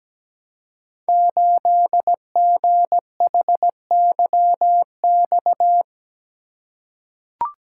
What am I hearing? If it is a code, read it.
Morse code, heard as a keyed tone, about 17 wpm, 700 Hz. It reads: 8GHYX
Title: Code Groups - 5 Letter:Numbers - 17wpm